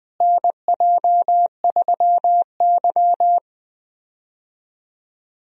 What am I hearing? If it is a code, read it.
Morse code, heard as a keyed tone, about 20 wpm, 700 Hz. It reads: NJ3Y
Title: USA Call Signs - 4 Characters - Repeated 3x - 20wpm